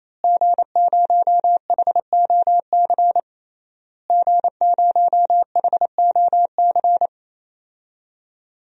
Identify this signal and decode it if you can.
Morse code, heard as a keyed tone, about 28 wpm, 700 Hz. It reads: G0HOC G0HOC
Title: DXCC-100 Call Signs - 5 Characters - Repeated 3x - 28wpm